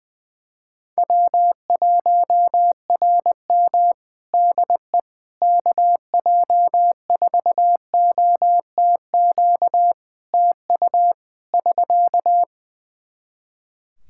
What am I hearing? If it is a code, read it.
Morse code, heard as a keyed tone, about 20 wpm, 700 Hz. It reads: W1RM DE KJ4OTQ TU <SK>
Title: Practice QSO - Exchange 4 - Ending - 20wpm